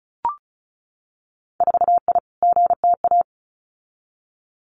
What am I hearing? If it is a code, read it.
Morse code, heard as a keyed tone, about 36 wpm, 700 Hz. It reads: 4I GTA
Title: Winter Field Day Exchanges Encore 35wpm